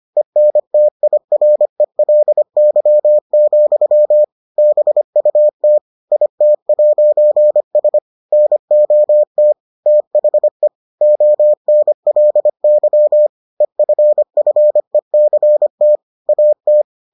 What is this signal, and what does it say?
Morse code, heard as a keyed tone, about 25 wpm, 600 Hz. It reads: ENTIRELY, BUT IT'S NOT THE ONLY EFFECT AT